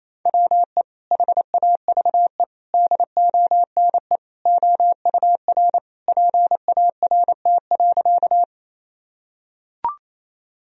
Morse code, heard as a keyed tone, about 28 wpm, 700 Hz. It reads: WE HAVE DONE OUR PART.
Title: Sentences from Top 1000 Words - Review Part 1 - 28wpm